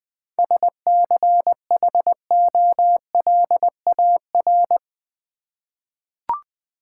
Morse code, heard as a keyed tone, about 20 wpm, 700 Hz. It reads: SCHOLAR
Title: 7 Letter Words - First 3 Letters Given - 20wpm